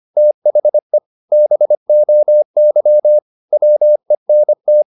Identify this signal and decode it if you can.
Morse code, heard as a keyed tone, about 25 wpm, 600 Hz. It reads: THE BOY WENT
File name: the old man and the sea part 1 25wpm